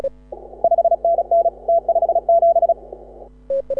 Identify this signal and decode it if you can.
Morse code, heard as a keyed tone, about 36 wpm, 680 Hz. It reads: 5NN T57
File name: AUDIO2018 / HA6OI CW